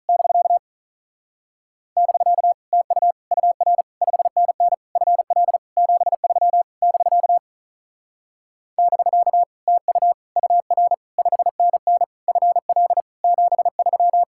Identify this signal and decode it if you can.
Morse code, heard as a keyed tone, about 43 wpm, 700 Hz. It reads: <BK> <BK> TU UR 5NN FL 73 <BK> <BK> TU UR 5NN FL 73
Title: Speed-Racing Common QSO Phrases - 35wpm